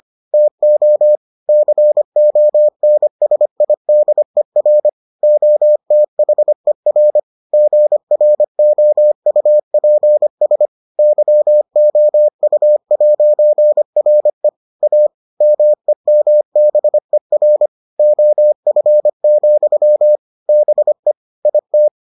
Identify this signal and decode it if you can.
Morse code, heard as a keyed tone, about 25 wpm, 600 Hz. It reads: TO CONSIDER OTHER GROUPS YOU'RE A MEMBER OF, BE IT